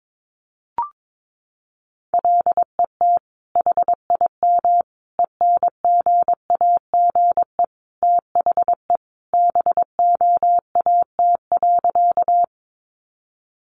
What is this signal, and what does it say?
Morse code, heard as a keyed tone, about 22 wpm, 700 Hz. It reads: LET HIM ENGAGE THE BOAT.
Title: Sentences from Top 1500 Words 22wpm